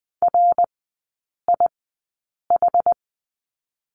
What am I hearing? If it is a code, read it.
Morse code, heard as a keyed tone, about 21 wpm, spaced as for 9 wpm, 690 Hz. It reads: RIH
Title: 145 Lesson 24e - Letter y - Three Characters 20wpm